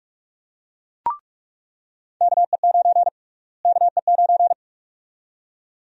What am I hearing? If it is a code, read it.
Morse code, heard as a keyed tone, about 44 wpm, 700 Hz. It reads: KE9 KE9
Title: USA Call Sign Prefixes - Repeated 2x - 45wpm